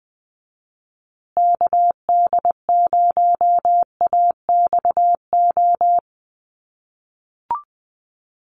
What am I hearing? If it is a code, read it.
Morse code, heard as a keyed tone, about 20 wpm, 700 Hz. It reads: KD0AXO
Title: USA Call Signs Repeated 2x 20wpm